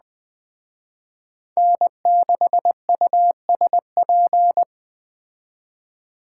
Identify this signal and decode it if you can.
Morse code, heard as a keyed tone, about 20 wpm, 700 Hz. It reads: N6USP
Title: Speed Racing - USA 1x3 Call Signs 17wpm